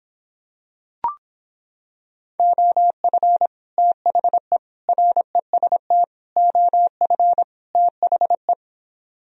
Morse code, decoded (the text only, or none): OF THE REST OF THE